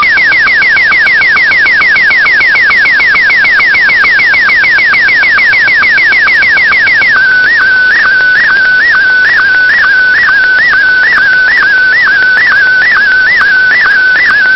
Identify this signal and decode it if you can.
SSTV (slow-scan television), Martin 1.
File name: qsl / sonidos / sstv martin1